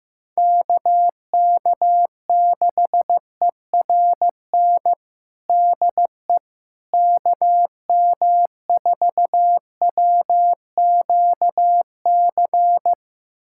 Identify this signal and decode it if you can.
Morse code, heard as a keyed tone, about 16 wpm, 700 Hz. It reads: KK6ERN DE KM4WQC
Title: USA Callsign DE Callsign - 6 Characters - 15wpm